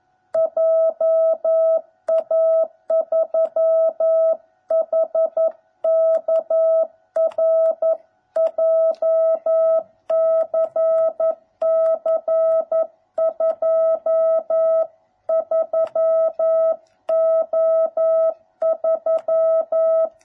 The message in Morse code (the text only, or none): JA3HKR JCC 23O3